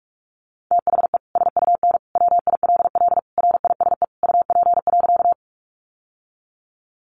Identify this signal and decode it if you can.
Morse code, heard as a keyed tone, about 44 wpm, 700 Hz. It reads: THE SUN WILL RISE UP.
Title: Sentences from Top 900 Words 45wpm